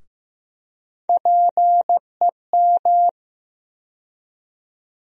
Morse code, heard as a keyed tone, about 15 wpm, 700 Hz. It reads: PEM
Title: USA Call Sign Postfixes - 3 Letters 15wpm